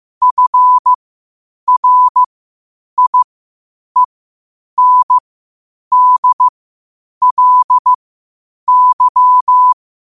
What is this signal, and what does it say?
Morse code, heard as a keyed tone, about 15 wpm, spaced as for 8 wpm, 1000 Hz. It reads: FRIENDLY
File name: qotdfiles / 8wpm